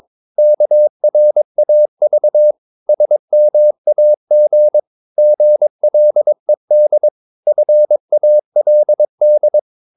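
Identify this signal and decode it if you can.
Morse code, heard as a keyed tone, about 22 wpm, 600 Hz. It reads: KRAV SMAG GLED FALD
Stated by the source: Morse practice of short Danish words at 22 wpm, from CW med Gnister.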